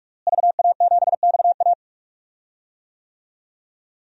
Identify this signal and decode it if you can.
Morse code, heard as a keyed tone, about 46 wpm, 700 Hz. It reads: VA7XA
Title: DXCC-100 Call Signs - 5 Characters - Repeated 3x - 45wpm